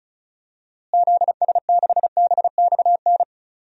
Morse code, heard as a keyed tone, about 36 wpm, 700 Hz. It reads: ZS6BXN
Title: DXCC-100 Call Signs - 6 Characters - Repeated 3x - 35wpm